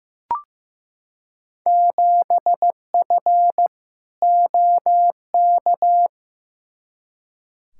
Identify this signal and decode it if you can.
Morse code, heard as a keyed tone, about 15 wpm, 700 Hz. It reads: 7F OK
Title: Field Day Exchanges Encore - 15wpm